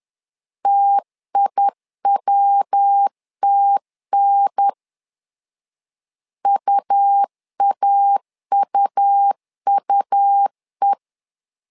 Morse code, heard as a keyed tone, about 10 wpm, 800 Hz. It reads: TIWTN UAUUE
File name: LEZIONI / 3 MWRO / 3 MW 10wpm 1959